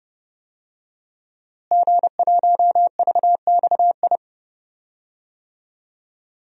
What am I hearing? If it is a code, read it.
Morse code, heard as a keyed tone, about 30 wpm, 700 Hz. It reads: G1VXI
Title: DXCC-100 Call Signs - 5 Characters - Repeated 3x - 30wpm